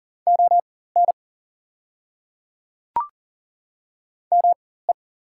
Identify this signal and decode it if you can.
Morse code, heard as a keyed tone, about 40 wpm, spaced as for 19 wpm, 690 Hz. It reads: ON ME
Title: 4x Character Spacing - Top 200 Words - 40wpm